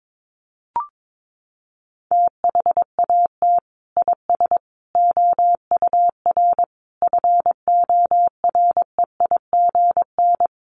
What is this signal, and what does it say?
Morse code, heard as a keyed tone, about 22 wpm, 700 Hz. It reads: THAT IS OUR FOREIGN
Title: Sentences from Top 1400 Words 22wpm